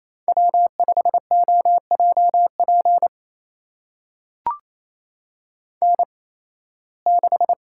Morse code, heard as a keyed tone, about 27 wpm, spaced as for 18 wpm, 700 Hz. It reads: W5OJP N6
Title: USA Call Signs - 5 Characters - 8x Character Spacing - 28wpm